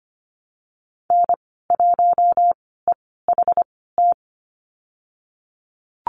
Morse code, heard as a keyed tone, about 25 wpm, spaced as for 16 wpm, 700 Hz. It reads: N1EHT
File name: Lesson 15g - Number 5 - Call Signs 25wpm